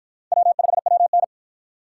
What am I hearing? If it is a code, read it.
Morse code, heard as a keyed tone, about 50 wpm, 700 Hz. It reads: W5WN